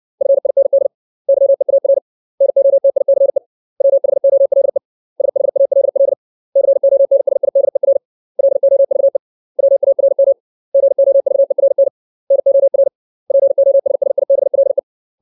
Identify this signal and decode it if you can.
Morse code, heard as a keyed tone, about 60 wpm, 550 Hz. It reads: FIND YEAR NOTICE WHOLE SHALL CONSIDER LOVE WARM GOVERN NOR POSSIBLE
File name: 632 words / 632 Words 60WPM